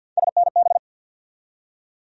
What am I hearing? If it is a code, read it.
Morse code, heard as a keyed tone, about 53 wpm, 700 Hz. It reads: SNB
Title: USA Call Sign Postfixes - 3 Letters 50wpm